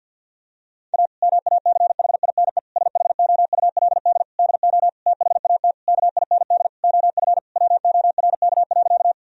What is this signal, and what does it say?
Morse code, heard as a keyed tone, about 50 wpm, 700 Hz. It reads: A MACHINE SHOULD DO THAT KIND OF WORK.
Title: Sentences from Top 1000 Words - Review Part 2 - 50wpm